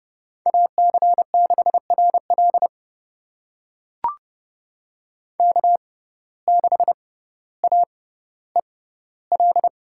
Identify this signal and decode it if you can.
Morse code, heard as a keyed tone, about 30 wpm, spaced as for 15 wpm, 700 Hz. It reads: AC6RL K6AEL
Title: USA Call Signs - 5 Characters - 6x Character Spacing - 30wpm